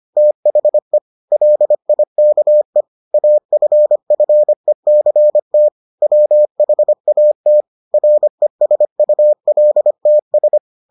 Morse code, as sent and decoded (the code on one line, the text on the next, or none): - .... / .-.. .. -.- . / .- ..-. ..-. -.-. - / .-- .... .- - / .-. ... ..- .-.. - ...
THE LIKE AFFECT WHAT RESULTS